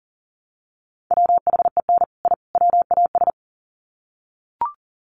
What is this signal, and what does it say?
Morse code, heard as a keyed tone, about 40 wpm, 700 Hz. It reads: WHEN I WAS
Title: Sets of 3 Words 40wpm